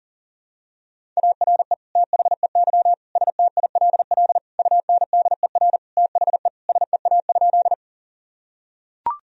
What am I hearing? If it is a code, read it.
Morse code, heard as a keyed tone, about 40 wpm, 700 Hz. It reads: ARE THEY STILL UNDER THE SEA?
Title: Sentences from Top 1000 Words - Review Part 1 - 40wpm